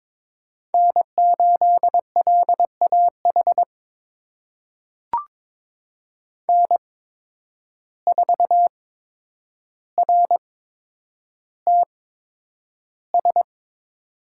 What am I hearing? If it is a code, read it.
Morse code, heard as a keyed tone, about 22 wpm, spaced as for 9 wpm, 700 Hz. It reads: N8LAH N4RTS
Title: USA Call Signs - 5 Characters - 8x Character Spacing - 22wpm